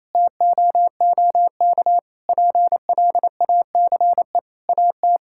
Morse code, heard as a keyed tone, about 28 wpm, 700 Hz. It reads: TOOK PLACE AT
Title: Sentences from Top 1600 Words 28wpm